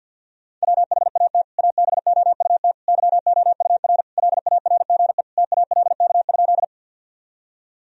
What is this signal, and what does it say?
Morse code, heard as a keyed tone, about 51 wpm, 700 Hz. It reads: WHAT ABOUT YOUR LARGE TALK?
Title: Sentences from Top 1000 Words - Review Part 1 - 50wpm